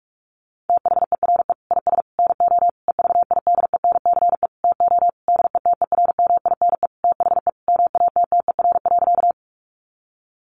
Morse code, heard as a keyed tone, about 44 wpm, 700 Hz. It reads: THERE IS NO EVIDENCE TO DETERMINE THE MATTER.
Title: Sentences from Top 1300 Words 45wpm